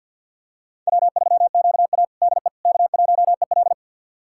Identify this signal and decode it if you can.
Morse code, heard as a keyed tone, about 52 wpm, 700 Hz. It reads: W3QA DE K1EL
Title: USA Callsign DE Callsign - 4 Characters - 50wpm